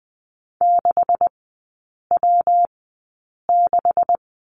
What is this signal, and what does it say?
Morse code, heard as a keyed tone, about 20 wpm, spaced as for 8 wpm, 700 Hz. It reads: 6W6
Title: 127 Lesson 21e - Number 6 - Three Characters 20wpm